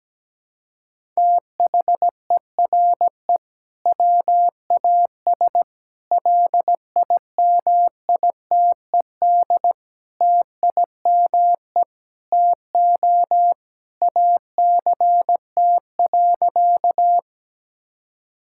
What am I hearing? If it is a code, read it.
Morse code, heard as a keyed tone, about 17 wpm, 700 Hz. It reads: THERE WAS LIMITED TIME TO ACT.
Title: Sentences from Top 1000 Words - Review Part 6 - 17wpm